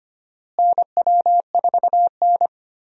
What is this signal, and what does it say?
Morse code, heard as a keyed tone, about 25 wpm, 700 Hz. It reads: NW4N
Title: USA Call Signs - 4 Characters - Repeated 2x - 25wpm